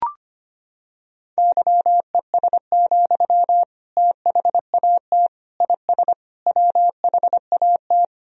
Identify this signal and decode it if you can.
Morse code, heard as a keyed tone, about 25 wpm, 700 Hz. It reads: YES, THAT IS WHAT
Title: Sentences from Top 500 Words and Abbreviations 25wpm